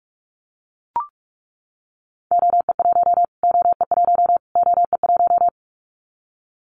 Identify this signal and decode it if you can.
Morse code, heard as a keyed tone, about 44 wpm, 700 Hz. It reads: OE1 OE1 OE1
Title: Callsign Prefixes for Top 25 DXCC - Repeated 3x - 45wpm